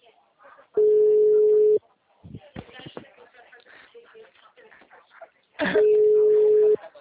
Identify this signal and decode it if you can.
Ringback tone (Europe, 425 Hz), with speech.
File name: out- -2009- - - .76943